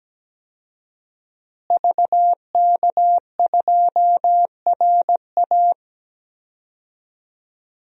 Morse code, heard as a keyed tone, about 17 wpm, 700 Hz. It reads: VK2RA
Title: DXCC-100 Call Signs - 5 Characters - Repeated 3x - 17wpm